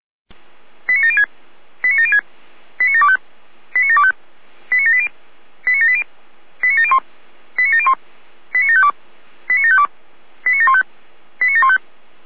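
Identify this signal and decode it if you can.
Five-tone selective call (ZVEI), heard as quick runs of five tones, tones of about 70 ms. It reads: 89706 89706 89725 89725 89780 89780 89701 89701 89752 89752 89715 89715
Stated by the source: ZVEI five-tone selective call received on an SDR by Feuerwehr Orlamünde.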